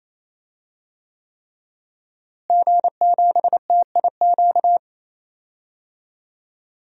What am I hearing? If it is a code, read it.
Morse code, heard as a keyed tone, about 28 wpm, 700 Hz. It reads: G7TIQ
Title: DXCC-100 Call Signs - 5 Characters - Repeated 3x - 28wpm